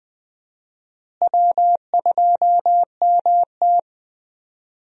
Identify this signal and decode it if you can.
Morse code, heard as a keyed tone, about 21 wpm, 700 Hz. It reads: W2MT
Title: USA Call Signs - 4 Characters - Repeated 2x - 20wpm